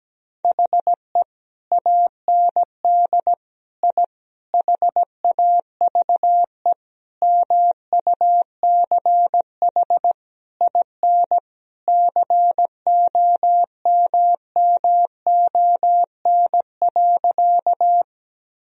Morse code, heard as a keyed tone, about 17 wpm, 700 Hz. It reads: HE AND I HAVE MUCH IN COMMON.
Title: Sentences from Top 1000 Words - Review Part 3 - 17wpm